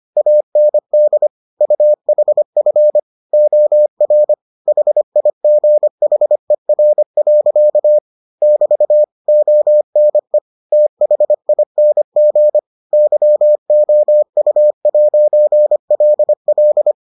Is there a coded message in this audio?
Yes: Morse code from 0.2 to 16.9 s, heard as a keyed tone, about 25 wpm, 600 Hz. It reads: AND UHF OR HIGHER. <BT> ONE THING YOU'LL